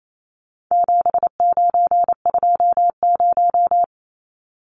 Morse code, heard as a keyed tone, about 28 wpm, 700 Hz. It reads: 7920
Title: Numbers - 4-Digit 28wpm